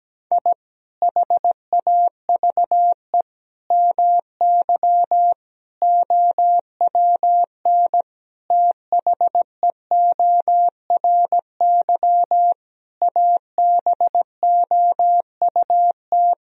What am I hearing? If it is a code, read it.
Morse code, heard as a keyed tone, about 17 wpm, 700 Hz. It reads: I HAVE MY OWN THEORY ABOUT
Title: Sentences from Top 1000 Words - Review Part 3 - 17wpm